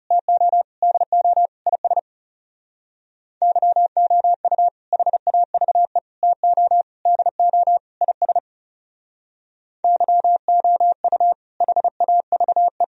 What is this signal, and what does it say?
Morse code, heard as a keyed tone, about 37 wpm, 700 Hz. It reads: TO DO IS YOU HAVE TO DO IS YOU HAVE
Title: Speed-Racing - Sets of 5 Words 30wpm